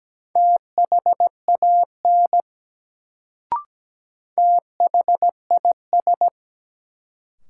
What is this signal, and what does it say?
Morse code, heard as a keyed tone, about 17 wpm, 700 Hz. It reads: THAN THIS
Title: Sets of 2 Words - New Word Drill - 17wpm